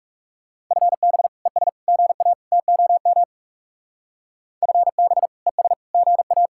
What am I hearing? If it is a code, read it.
Morse code, heard as a keyed tone, about 36 wpm, 700 Hz. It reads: FB ES GA TOM FB ES GA
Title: Speed-Racing Common QSO Phrases - 30wpm